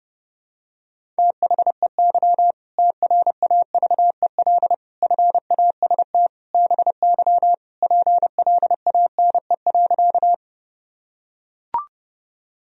Morse code, heard as a keyed tone, about 30 wpm, 700 Hz. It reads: THEY TRAVEL FAST BY PLANE.
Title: Sentences from Top 1000 Words - Review Part 3 - 30wpm